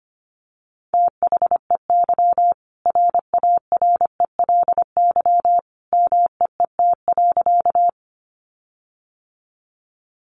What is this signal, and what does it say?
Morse code, heard as a keyed tone, about 25 wpm, 700 Hz. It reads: THEY RARELY MEET.
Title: Sentences from Top 1400 Words 25wpm